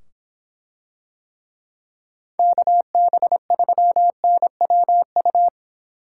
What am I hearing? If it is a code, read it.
Morse code, heard as a keyed tone, about 24 wpm, 700 Hz. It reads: KB3NWU